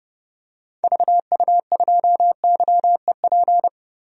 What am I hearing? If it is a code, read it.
Morse code, heard as a keyed tone, about 30 wpm, 700 Hz. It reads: VU2YEP
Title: DXCC-100 Call Signs - 6 Characters - Repeated 2x - 30wpm